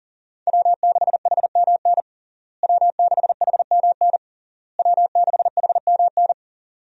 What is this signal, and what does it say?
Morse code, heard as a keyed tone, about 40 wpm, 700 Hz. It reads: W6HMN W6HMN W6HMN
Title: USA Call Signs - 5 Characters - Repeated 3x - 40wpm